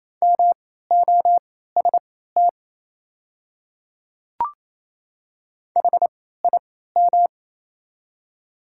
Morse code, heard as a keyed tone, about 28 wpm, spaced as for 16 wpm, 700 Hz. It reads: MOST HIM
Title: 3x Character Spacing - Top 100 Words - 28wpm